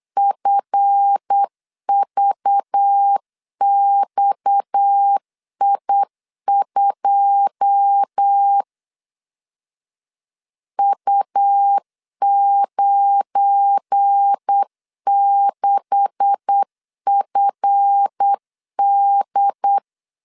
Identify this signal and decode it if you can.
Morse code, heard as a keyed tone, about 8 wpm, 800 Hz. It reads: FVXI2 U96FD
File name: LEZIONI / 10 SIGNS / 10B INT 8wpm 3534